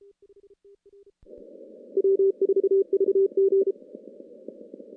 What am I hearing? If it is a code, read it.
Morse code, heard as a keyed tone, about 33 wpm, 390 Hz. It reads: W4VG
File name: contests / CW-SS-k5tr- -2300